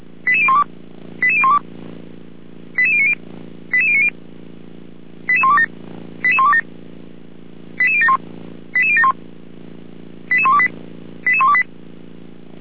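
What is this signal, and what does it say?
Five-tone selective call (ZVEI), heard as quick runs of five tones, tones of about 70 ms. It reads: 88012 88012 88090 88090 88127 88127 88071 88071 88128 88128
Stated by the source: ZVEI five-tone call received on an SDR.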